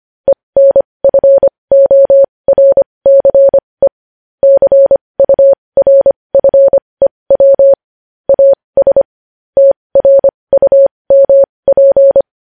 Morse code, heard as a keyed tone, about 25 wpm, spaced as for 20 wpm, 550 Hz. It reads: ENFORCE CURFEW AS TRUMP
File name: podcast / News headlines-1x-25x20